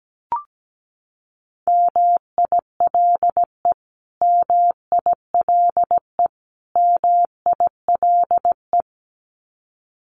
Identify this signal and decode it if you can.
Morse code, heard as a keyed tone, about 17 wpm, 700 Hz. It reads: MILE MILE MILE
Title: Top 300 Words Repeated 17wpm